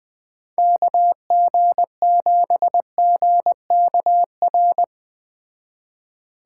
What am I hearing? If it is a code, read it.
Morse code, heard as a keyed tone, about 21 wpm, 700 Hz. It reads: KG7GKR